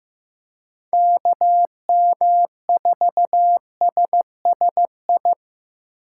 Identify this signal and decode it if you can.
Morse code, heard as a keyed tone, about 15 wpm, 700 Hz. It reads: KM4SSI
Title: USA Call Signs - 6 Characters - 2x Character Spacing - 15wpm